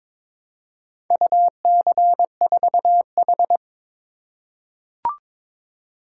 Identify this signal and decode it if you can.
Morse code, heard as a keyed tone, about 22 wpm, 690 Hz. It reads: UC4H
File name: practice / 22wpm / Four Letter-Number Combinations - 22wpm